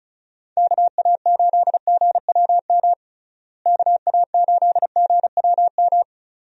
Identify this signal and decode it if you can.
Morse code, heard as a keyed tone, about 36 wpm, 700 Hz. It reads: KA8GWM KA8GWM
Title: USA Call Signs - 6 Characters - Repeated 2x - 35wpm